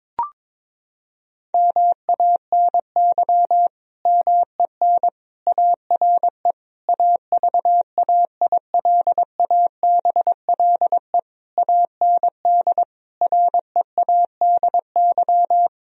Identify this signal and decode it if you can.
Morse code, heard as a keyed tone, about 22 wpm, 700 Hz. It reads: MANY MEN ARE AVAILABLE AND READY
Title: Sentences from Top 1000 Words - Review Part 3 - 22wpm